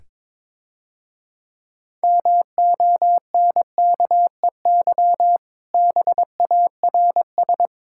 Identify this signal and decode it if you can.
Morse code, heard as a keyed tone, about 23 wpm, 700 Hz. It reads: MONKEY BARS
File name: Easy Words - Speed-Racing with Hints - 22wpm